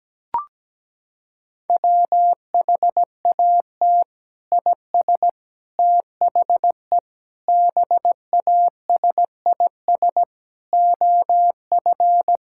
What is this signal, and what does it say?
Morse code, heard as a keyed tone, about 17 wpm, 700 Hz. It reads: WHAT IS THE BASIS OF
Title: Sentences from Top 1000 Words - Review Part 5 - 17wpm